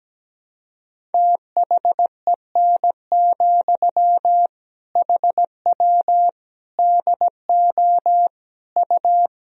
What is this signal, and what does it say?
Morse code, heard as a keyed tone, about 17 wpm, 700 Hz. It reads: THEN, HW DO U
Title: Sentences from Top 100 Words and Abbreviations 17wpm